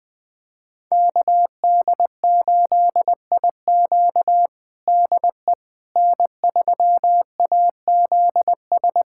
Morse code, heard as a keyed tone, about 20 wpm, 700 Hz. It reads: KD8IQ DE N3AZS
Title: USA Callsign DE Callsign - 5 Characters - 20wpm